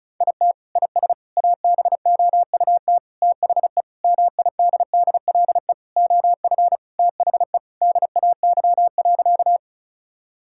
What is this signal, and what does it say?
Morse code, heard as a keyed tone, about 36 wpm, 700 Hz. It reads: IT IS ABOUT THE MIDDLE OF THE DAY.
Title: Sentences from Top 1000 Words - Review Part 6 - 35wpm